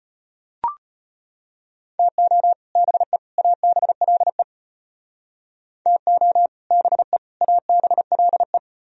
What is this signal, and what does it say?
Morse code, heard as a keyed tone, about 32 wpm, 700 Hz. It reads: TO BE ABLE TO BE ABLE